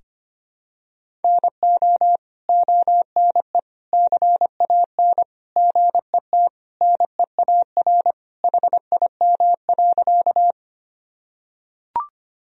Morse code, heard as a keyed tone, about 25 wpm, 700 Hz. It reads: NO ONE CAN GET NEAR HIM.
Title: Sentences from Top 1000 Words - Review Part 1 - 25wpm